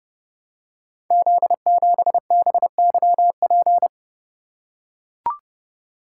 Morse code, heard as a keyed tone, about 31 wpm, 700 Hz. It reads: Z7BYP
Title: Code Groups - 5 Letter:Numbers - 30wpm